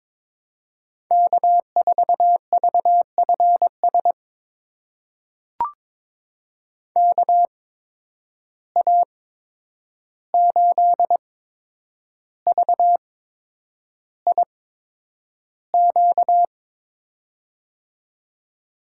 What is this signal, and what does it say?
Morse code, heard as a keyed tone, about 22 wpm, spaced as for 9 wpm, 700 Hz. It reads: K4VFS KA8VIQ
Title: USA Call Signs - 8x Character Spacing - 22wpm